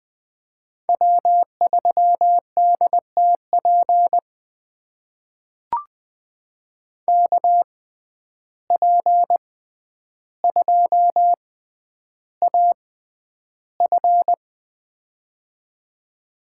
Morse code, heard as a keyed tone, about 20 wpm, spaced as for 10 wpm, 700 Hz. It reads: W3DTP KP2AF
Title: USA Call Signs - 5 Characters - 6x Character Spacing - 20wpm